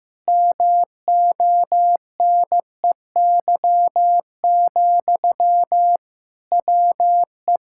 Morse code, heard as a keyed tone, about 15 wpm, 700 Hz. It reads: MONEY, WE
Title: Sentences from Top 1600 Words 15wpm